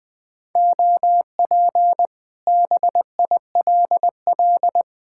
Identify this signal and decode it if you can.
Morse code, heard as a keyed tone, about 20 wpm, 700 Hz. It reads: OP BILL
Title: Speed-Racing Common QSO Phrases Encore - 20wpm